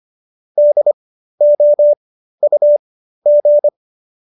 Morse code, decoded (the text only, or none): DOUG